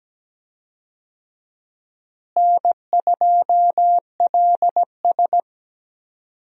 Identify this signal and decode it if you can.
Morse code, heard as a keyed tone, about 17 wpm, 700 Hz. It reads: N2LS
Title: USA Call Signs - 4 Characters - Repeated 2x - 17wpm